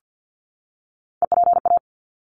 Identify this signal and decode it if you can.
Morse code, heard as a keyed tone, about 49 wpm, 700 Hz. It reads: EPA
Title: ARRL Sections 50wpm